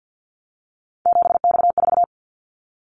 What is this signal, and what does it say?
Morse code, heard as a keyed tone, about 50 wpm, 700 Hz. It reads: 7X4
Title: Callsign Prefixes for Top 100 DXCC - 50wpm